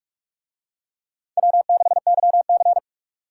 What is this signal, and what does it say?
Morse code, heard as a keyed tone, about 43 wpm, 700 Hz. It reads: W6YC